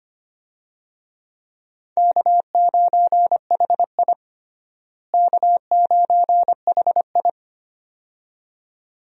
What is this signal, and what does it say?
Morse code, heard as a keyed tone, about 25 wpm, 700 Hz. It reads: K9HI K9HI
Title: USA Call Signs - 4 Characters - Repeated 2x - 25wpm